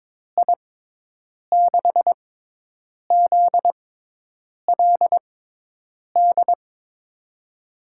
Morse code, heard as a keyed tone, about 22 wpm, spaced as for 8 wpm, 700 Hz. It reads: I6ZLD
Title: USA Call Signs - 6 Characters - 6x Character Spacing - 22wpm